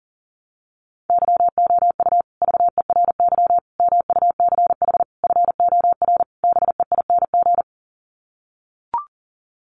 Morse code, heard as a keyed tone, about 37 wpm, 700 Hz. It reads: YOU VERY MUCH FOR BEING
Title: Sets of 5 Words Encore 40wpm